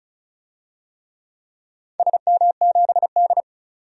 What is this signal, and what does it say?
Morse code, heard as a keyed tone, about 34 wpm, 700 Hz. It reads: SM7D
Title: DXCC-100 Call Signs - 4 Characters - 35wpm